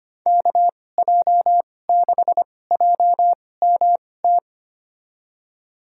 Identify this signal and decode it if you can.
Morse code, heard as a keyed tone, about 25 wpm, spaced as for 18 wpm, 700 Hz. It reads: KJ6JMT
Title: USA Call Signs - 6 Characters - 2x Character Spacing - 25wpm